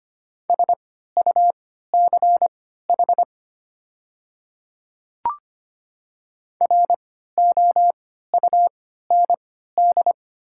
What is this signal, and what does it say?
Morse code, heard as a keyed tone, about 25 wpm, spaced as for 14 wpm, 700 Hz. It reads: SUCH ROUND
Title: 3x Character Spacing - Top 300 Words - 25wpm